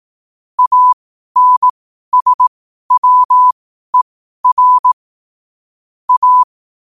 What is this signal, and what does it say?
Morse code, heard as a keyed tone, about 18 wpm, spaced as for 12 wpm, 1000 Hz. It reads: ANSWER A
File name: qotdfiles / 12wpm